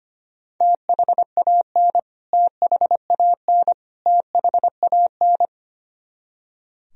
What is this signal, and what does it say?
Morse code, heard as a keyed tone, about 25 wpm, 700 Hz. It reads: THAN THAN THAN
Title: Top 100 Words Repeated 25wpm